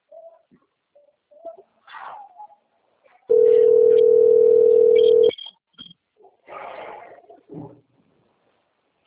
Ringback tone (North America), 440 + 480 Hz.